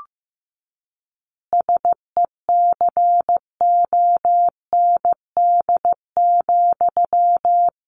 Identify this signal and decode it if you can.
Morse code, heard as a keyed tone, about 15 wpm, 700 Hz. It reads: SECOND,
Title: Easy Phrases 15wpm